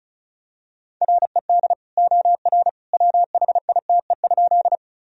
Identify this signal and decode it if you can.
Morse code, heard as a keyed tone, about 36 wpm, 700 Hz. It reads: RED OR WHITE?
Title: Sentences from Top 1000 Words - Review Part 2 - 35wpm